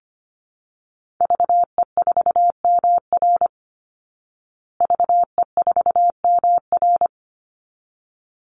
Morse code, heard as a keyed tone, about 25 wpm, 700 Hz. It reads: VE4MR VE4MR
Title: Call Signs Repeated 3x - 25wpm